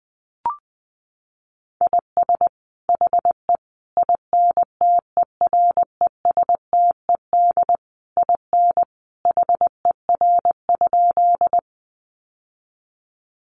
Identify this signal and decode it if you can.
Morse code, heard as a keyed tone, about 19 wpm, 700 Hz. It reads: IS HE INTERESTED IN HER?
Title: Sentences from Top 900 Words 20wpm